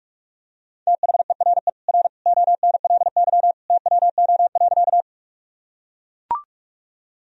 Morse code, heard as a keyed tone, about 41 wpm, 700 Hz. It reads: THERE R ONLY TWO.
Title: Sentences from Top 200 Words and Abbreviations 45wpm